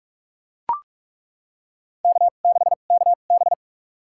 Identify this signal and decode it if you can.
Morse code, heard as a keyed tone, about 42 wpm, spaced as for 31 wpm, 700 Hz. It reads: K6KB
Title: USA Call Signs - 2x Character Spacing - 45wpm